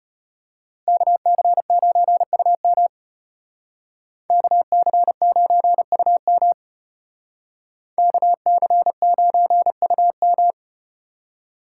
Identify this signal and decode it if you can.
Morse code, heard as a keyed tone, about 35 wpm, 700 Hz. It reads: KC9UM KC9UM KC9UM